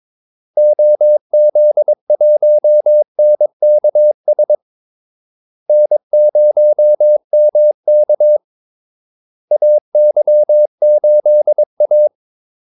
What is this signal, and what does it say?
Morse code, heard as a keyed tone, about 22 wpm, 600 Hz. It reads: OZ1NKS N0MK AY8A
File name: Callsigns-22wpm